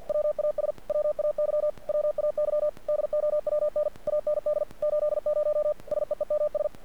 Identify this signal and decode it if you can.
Morse code, heard as a keyed tone, about 48 wpm, 600 Hz. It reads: WAS WAY WAY DOWN AND 80 SEEMS
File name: archived / CRSnet